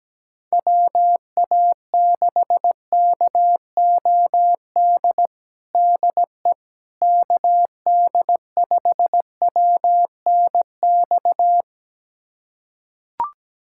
Morse code, heard as a keyed tone, about 17 wpm, 700 Hz. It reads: WA6KOD DE KD5WNX
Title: USA Callsign DE Callsign - 6 Characters - 17wpm